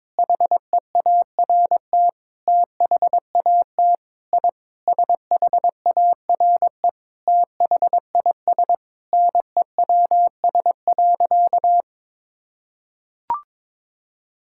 Morse code, heard as a keyed tone, about 22 wpm, 700 Hz. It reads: HEART THAT I SHARE THIS NEWS.
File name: practice / 22wpm / Sentences with 8-Word Phrases - 22wpm